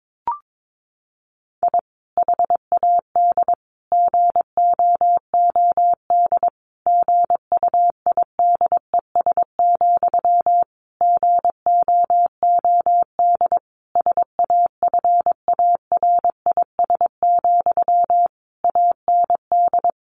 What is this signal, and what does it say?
Morse code, heard as a keyed tone, about 22 wpm, 700 Hz. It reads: I HAD GOOD GUIDES, GOOD SAFARIS, AND
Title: A Sound Of Thunder 22wpm